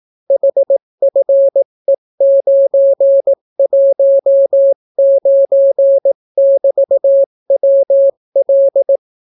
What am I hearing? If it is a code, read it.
Morse code, heard as a keyed tone, about 18 wpm, 550 Hz. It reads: HFE919<BT>WL